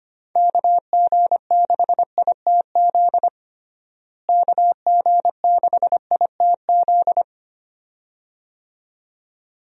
Morse code, heard as a keyed tone, about 25 wpm, 700 Hz. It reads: KG6ITZ KG6ITZ
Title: USA Call Signs - 6 Characters - Repeated 2x - 25wpm